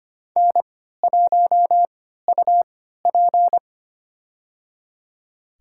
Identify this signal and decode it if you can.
Morse code, heard as a keyed tone, about 25 wpm, spaced as for 14 wpm, 700 Hz. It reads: N1UP
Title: USA Call Signs - 4 Characters - 3x Character Spacing - 25wpm